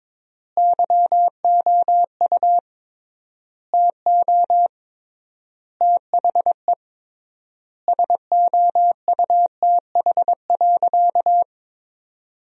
Morse code, heard as a keyed tone, about 22 wpm, 700 Hz. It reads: YOU TO THE SOUTH.
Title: 3x Word-Spacing - Sentences from Top 400 Words - 22wpm